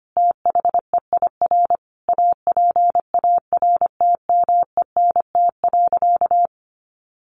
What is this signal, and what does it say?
Morse code, heard as a keyed tone, about 25 wpm, 700 Hz. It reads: THEIR APARTMENT.
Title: Sentences from Top 900 Words 25wpm